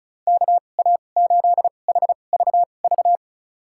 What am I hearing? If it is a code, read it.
Morse code, heard as a keyed tone, about 36 wpm, spaced as for 25 wpm, 700 Hz. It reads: KA8HVV
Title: USA Call Signs - 2x Character Spacing - 35wpm